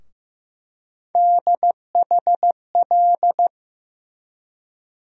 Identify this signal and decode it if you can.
Morse code, heard as a keyed tone, about 15 wpm, 700 Hz. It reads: DHL
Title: USA Call Sign Postfixes - 3 Letters 15wpm